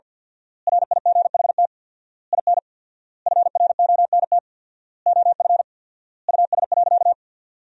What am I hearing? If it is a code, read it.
Morse code, heard as a keyed tone, about 47 wpm, 700 Hz. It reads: RIGHT IN FRONT OF US.
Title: 4x Word-Spacing - Sentences from Top 500 Words - 50wpm